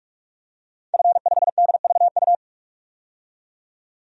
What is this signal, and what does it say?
Morse code, heard as a keyed tone, about 46 wpm, 700 Hz. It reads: F5DVU
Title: DXCC-100 Call Signs - 5 Characters - 45wpm